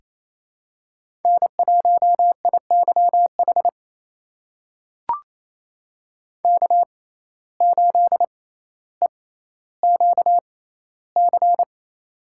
Morse code, heard as a keyed tone, about 28 wpm, spaced as for 14 wpm, 700 Hz. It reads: N1IYH K8EQC
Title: USA Call Signs - 5 Characters - 6x Character Spacing - 28wpm